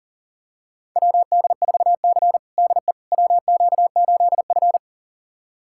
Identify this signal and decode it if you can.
Morse code, heard as a keyed tone, about 40 wpm, 700 Hz. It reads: WD4C DE WQ8F
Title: USA Callsign DE Callsign - 4 Characters - 40wpm